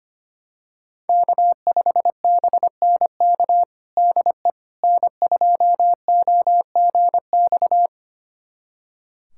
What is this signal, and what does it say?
Morse code, heard as a keyed tone, about 25 wpm, 700 Hz. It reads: K5BNK DE N2OGX
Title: USA Callsign DE Callsign - 5 Characters - 25wpm